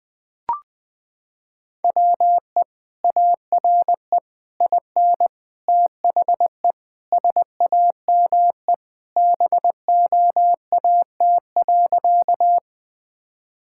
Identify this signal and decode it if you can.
Morse code, heard as a keyed tone, about 21 wpm, 700 Hz. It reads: WE ARE IN THE SAME BOAT.
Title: Sentences with 3-Word Phrases - 20wpm